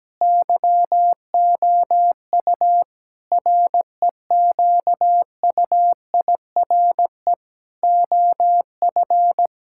Morse code, heard as a keyed tone, about 17 wpm, 700 Hz. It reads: YOU REQUIRE OF
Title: Sentences from Top 1000 Words - Review Part 6 - 17wpm